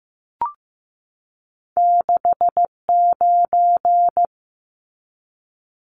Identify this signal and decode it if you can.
Morse code, heard as a keyed tone, about 15 wpm, 700 Hz. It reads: 69
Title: Numbers - 2-Digit 15wpm